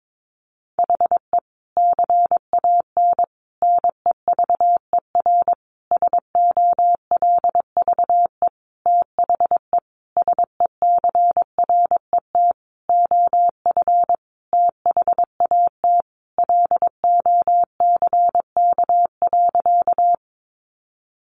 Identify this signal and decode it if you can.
Morse code, heard as a keyed tone, about 22 wpm, 700 Hz. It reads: HE CAN NEVER SOLVE THE SECRET OF THAT LOCK.
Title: Sentences from Top 1300 Words 22wpm